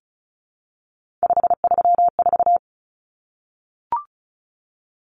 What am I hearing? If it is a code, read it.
Morse code, heard as a keyed tone, about 34 wpm, 700 Hz. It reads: 534
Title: Numbers - 3-Digit 35wpm